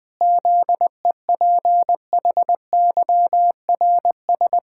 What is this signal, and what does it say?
Morse code, heard as a keyed tone, about 20 wpm, 700 Hz. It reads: ZEPHYRS
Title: Pangram Challenge v2 - 20wpm